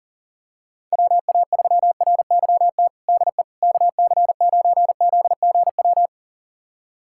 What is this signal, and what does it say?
Morse code, heard as a keyed tone, about 40 wpm, 700 Hz. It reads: WA3RYT DE KC9ZGW
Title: USA Callsign DE Callsign 40wpm